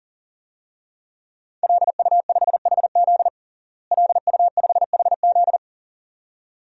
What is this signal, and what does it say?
Morse code, heard as a keyed tone, about 40 wpm, 700 Hz. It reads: LU5HZ LU5HZ
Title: DXCC-100 Call Signs - 5 Characters - Repeated 2x - 40wpm